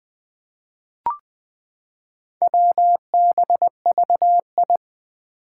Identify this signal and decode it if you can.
Morse code, heard as a keyed tone, about 20 wpm, 700 Hz. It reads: WBVI
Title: Four Challenging Letter-Number Combinations 20wpm